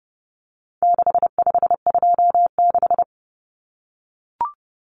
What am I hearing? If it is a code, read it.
Morse code, heard as a keyed tone, about 30 wpm, 700 Hz. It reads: 6526
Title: Numbers - 4-Digit 30wpm